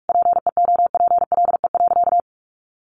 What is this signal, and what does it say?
Morse code, heard as a keyed tone, about 44 wpm, 700 Hz. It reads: PEOPLE.
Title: Sentences from Top 1500 Words 45wpm